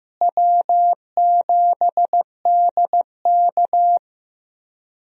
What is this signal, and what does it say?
Morse code, heard as a keyed tone, about 15 wpm, 700 Hz. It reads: W7DK
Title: USA Call Signs - 4 Characters - Repeated 2x - 15wpm